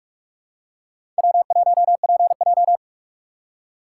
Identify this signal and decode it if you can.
Morse code, heard as a keyed tone, about 44 wpm, 690 Hz. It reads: W1PJ
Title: USA Call Signs - 4 Characters - 3x Character Spacing - 45wpm